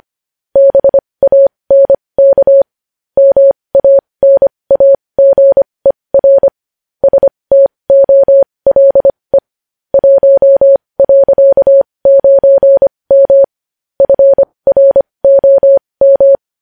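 Morse code, heard as a keyed tone, about 25 wpm, spaced as for 20 wpm, 550 Hz. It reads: BANK MANAGER STOLE 1.9M FROM
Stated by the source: Morse code at 20 wpm.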